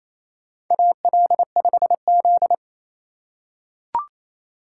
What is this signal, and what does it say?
Morse code, heard as a keyed tone, about 28 wpm, 700 Hz. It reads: AL5Z
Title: USA Call Signs - 4 Characters - 3x Character Spacing - 28wpm